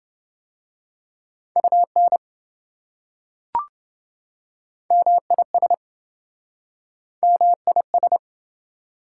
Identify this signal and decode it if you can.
Morse code, heard as a keyed tone, about 27 wpm, 700 Hz. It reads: UN MIS MIS